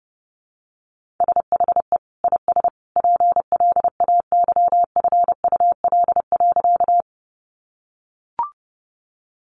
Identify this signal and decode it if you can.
Morse code, heard as a keyed tone, about 30 wpm, 700 Hz. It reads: SHE IS PLAYFUL.
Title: Easy Phrases 30wpm